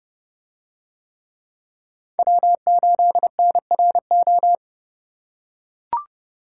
Morse code, heard as a keyed tone, about 28 wpm, 700 Hz. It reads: W8NRO